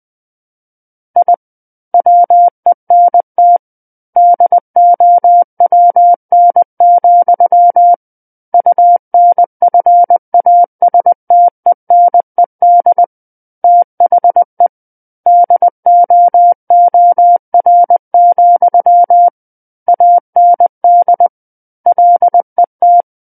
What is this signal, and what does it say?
Morse code, heard as a keyed tone, about 20 wpm, 700 Hz. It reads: I WENT DOWN, UNFASTENED THE DOOR, AND LET
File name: 20 WPM